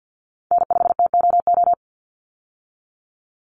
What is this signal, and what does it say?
Morse code, heard as a keyed tone, about 50 wpm, 700 Hz. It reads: N5TOO
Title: Call Signs Repeated 2x - 50wpm